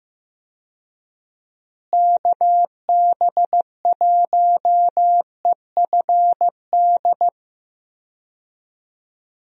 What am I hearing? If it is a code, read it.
Morse code, heard as a keyed tone, about 15 wpm, 700 Hz. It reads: KB1EFD
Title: USA Call Signs - 6 Characters - Repeated 2x - 15wpm